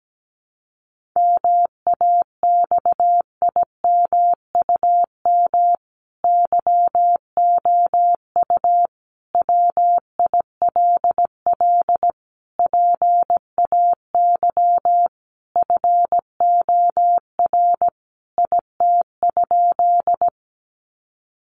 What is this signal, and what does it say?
Morse code, heard as a keyed tone, about 17 wpm, 700 Hz. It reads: MAXIMUM YOU WILL PAY FOR IT?
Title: Sentences from Top 1300 Words 17wpm